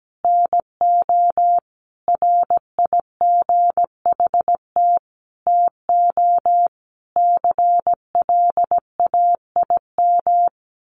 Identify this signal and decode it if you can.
Morse code, heard as a keyed tone, about 17 wpm, 700 Hz. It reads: NO RIGHT TO CLAIM
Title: Sentences from Top 1500 Words 17wpm